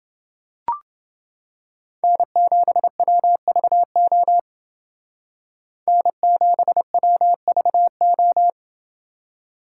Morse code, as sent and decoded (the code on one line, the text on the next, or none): -. --... .-- ...- --- / -. --... .-- ...- ---
N7WVO N7WVO